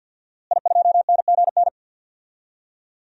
Morse code, heard as a keyed tone, about 52 wpm, 700 Hz. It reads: I2NGN
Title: DXCC-100 Call Signs - 5 Characters - Repeated 2x - 50wpm